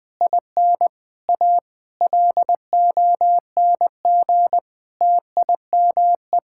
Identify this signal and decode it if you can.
Morse code, heard as a keyed tone, about 20 wpm, 700 Hz. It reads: IN A LONG TIME
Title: Sets of 8 Words - 20wpm